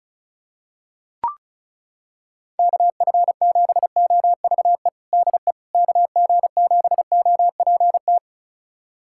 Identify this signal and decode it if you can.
Morse code, heard as a keyed tone, about 36 wpm, 700 Hz. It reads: KF7OVE DE KG7OPT
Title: USA Callsign DE Callsign - 6 Characters - 35wpm